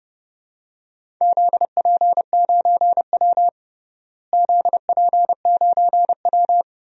Morse code, heard as a keyed tone, about 30 wpm, 700 Hz. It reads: ZP9W ZP9W
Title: DXCC-100 Call Signs - 4 Characters - Repeated 2x - 30wpm